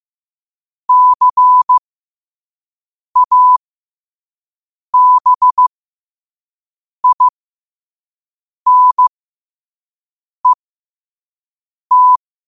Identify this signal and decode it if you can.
Morse code, heard as a keyed tone, about 15 wpm, spaced as for 5 wpm, 1000 Hz. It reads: CABINET
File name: qotdfiles / 5wpm